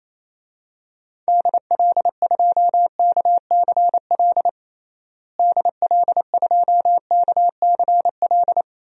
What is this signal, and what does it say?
Morse code, heard as a keyed tone, about 28 wpm, 700 Hz. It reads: DL2KCL DL2KCL
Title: DXCC-100 Call Signs - 6 Characters - Repeated 3x - 28wpm